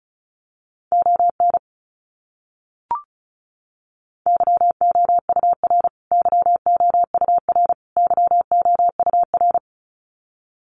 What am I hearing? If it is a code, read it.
Morse code, heard as a keyed tone, about 34 wpm, 700 Hz. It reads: ON YOUR YOUR YOUR
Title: Top 200 Words Repeated 35wpm